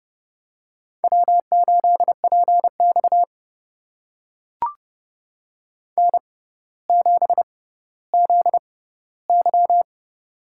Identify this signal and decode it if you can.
Morse code, heard as a keyed tone, about 29 wpm, spaced as for 15 wpm, 700 Hz. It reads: W8PX N7ZY
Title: USA Call Signs - 4 Characters - 6x Character Spacing - 30wpm